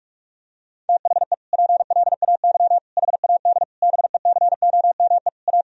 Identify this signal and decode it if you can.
Morse code, heard as a keyed tone, about 46 wpm, 700 Hz. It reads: THE PLAY HAD BECOME A
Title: Sentences from Top 1000 Words - Review Part 4 - 45wpm